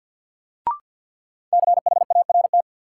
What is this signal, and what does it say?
Morse code, heard as a keyed tone, about 50 wpm, 700 Hz. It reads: CHART
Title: Sentences from Top 1300 Words - New Word Drill Alt - 50wpm